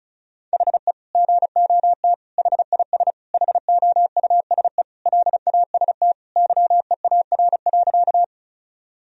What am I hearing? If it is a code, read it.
Morse code, heard as a keyed tone, about 36 wpm, 690 Hz. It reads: HE GOT HIS HOUSE LAST YEAR.
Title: Sentences from Top 1000 Words - Review Part 1 - 35wpm